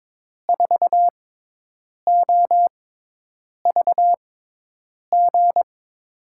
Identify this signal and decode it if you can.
Morse code, heard as a keyed tone, about 22 wpm, spaced as for 8 wpm, 700 Hz. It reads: 4OVG
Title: USA Call Signs - 6 Characters - 6x Character Spacing - 22wpm